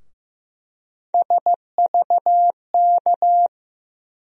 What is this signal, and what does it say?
Morse code, heard as a keyed tone, about 15 wpm, 700 Hz. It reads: SVK
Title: USA Call Sign Postfixes - 3 Letters 15wpm